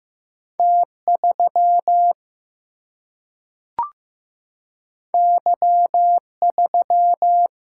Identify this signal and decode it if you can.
Morse code, heard as a keyed tone, about 15 wpm, 700 Hz. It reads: T3 Y3
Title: Single-Letter Single-Number 15wpm